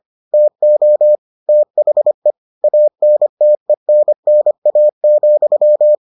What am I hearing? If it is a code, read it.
Morse code, heard as a keyed tone, about 25 wpm, 600 Hz. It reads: TO THE ANTENNA,